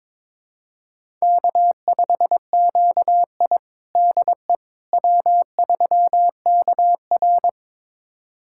Morse code, heard as a keyed tone, about 22 wpm, 700 Hz. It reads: K5QI DE W3KR
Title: USA Callsign DE Callsign - 4 Characters - 22wpm